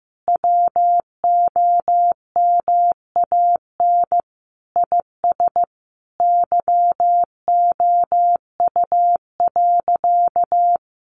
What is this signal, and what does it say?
Morse code, heard as a keyed tone, about 15 wpm, 700 Hz. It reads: WOMAN IS YOU.
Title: Sentences from Top 1100 Words 15wpm